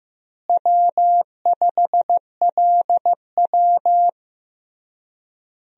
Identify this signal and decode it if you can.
Morse code, heard as a keyed tone, about 16 wpm, 700 Hz. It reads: W5LW
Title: USA Call Signs - 4 Characters - Repeated 3x - 15wpm